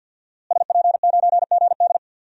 Morse code, heard as a keyed tone, about 53 wpm, 700 Hz. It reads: SP9GD
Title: DXCC-100 Call Signs - 5 Characters - Repeated 3x - 50wpm